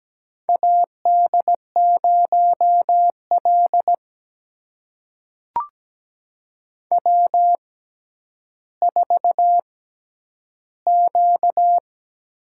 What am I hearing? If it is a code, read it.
Morse code, heard as a keyed tone, about 17 wpm, spaced as for 10 wpm, 700 Hz. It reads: AD0L W4Q
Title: USA Call Signs - 4 Characters - 6x Character Spacing - 17wpm